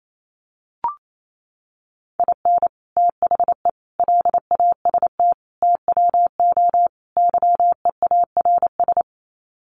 Morse code, heard as a keyed tone, about 28 wpm, 700 Hz. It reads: IN THE LAST TWO YEARS
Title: Sets of 5 Words 28wpm